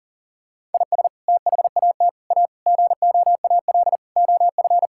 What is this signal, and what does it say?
Morse code, heard as a keyed tone, about 40 wpm, 700 Hz. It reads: IS THAT A GOAL OF
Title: Sentences from Top 1000 Words - Review Part 5 - 40wpm